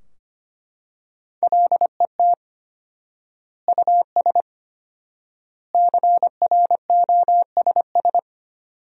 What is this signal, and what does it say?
Morse code, heard as a keyed tone, about 25 wpm, 700 Hz. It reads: LET US CROSS
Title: 4x Word-Spacing - Sentences from Top 300 Words - 25wpm